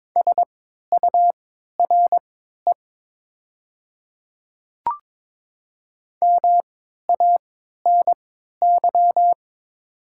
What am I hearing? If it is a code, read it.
Morse code, heard as a keyed tone, about 22 wpm, spaced as for 13 wpm, 700 Hz. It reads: SURE MANY
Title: 3x Character Spacing - Top 500 Words - 22wpm